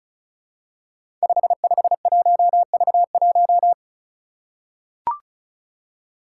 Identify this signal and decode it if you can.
Morse code, heard as a keyed tone, about 36 wpm, 700 Hz. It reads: 551V1
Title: Five Challenging Letter-Number Combinations 35wpm